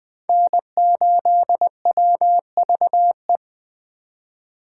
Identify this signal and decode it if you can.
Morse code, heard as a keyed tone, about 20 wpm, 700 Hz. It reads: N8WVE
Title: Speed Racing - USA 1x3 Call Signs 17wpm